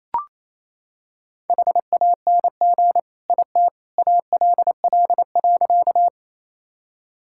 Morse code, heard as a keyed tone, about 27 wpm, 700 Hz. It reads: HANG IT ALL.
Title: Sentences from Top 1600 Words 28wpm